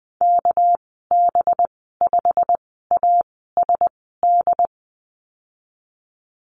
Morse code, heard as a keyed tone, about 20 wpm, spaced as for 15 wpm, 700 Hz. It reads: KB5ASD
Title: 213 Lesson 34g - Number 8 - Call Signs 20wpm